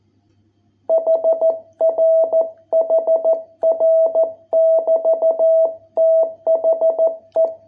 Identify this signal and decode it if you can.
Morse code, heard as a keyed tone, about 14 wpm, 660 Hz. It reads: HR HR <BT> THE